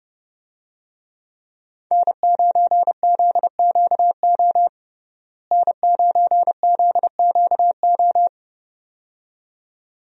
Morse code, heard as a keyed tone, about 30 wpm, 700 Hz. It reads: N9ZQO N9ZQO
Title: USA Call Signs - 5 Characters - Repeated 2x - 30wpm